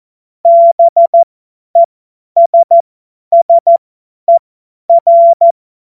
Morse code, heard as a keyed tone, about 14 wpm, spaced as for 10 wpm, 680 Hz. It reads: BESSER
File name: Morsekurs / Witze / Witz 27. Juli 2021, 50 F70